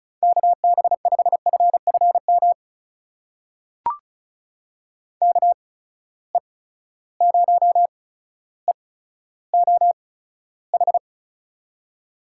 Morse code, heard as a keyed tone, about 36 wpm, spaced as for 15 wpm, 700 Hz. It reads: KB5FFM KE0EOH
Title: USA Call Signs - 6 Characters - 8x Character Spacing - 35wpm